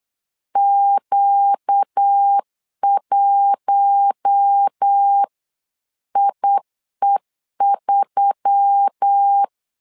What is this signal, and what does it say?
Morse code, heard as a keyed tone, about 8 wpm, 800 Hz. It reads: Q1IE3
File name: LEZIONI / 5 LBVC / 5 LB 8wpm 2816